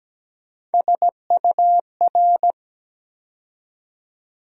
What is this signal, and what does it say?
Morse code, heard as a keyed tone, about 17 wpm, 700 Hz. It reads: SUR
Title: USA Call Sign Postfixes - 3 Letters 17wpm